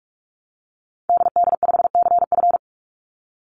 Morse code, heard as a keyed tone, about 44 wpm, 700 Hz. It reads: DD5CF
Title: Call Signs Repeated 3x - 45wpm